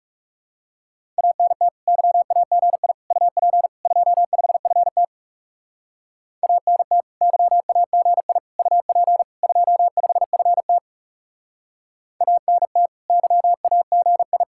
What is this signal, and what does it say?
Morse code, heard as a keyed tone, about 36 wpm, 700 Hz. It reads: ANT YAGI UP 25FT ANT YAGI UP 25FT ANT YAGI
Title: Speed-Racing Common QSO Phrases Encore - 30wpm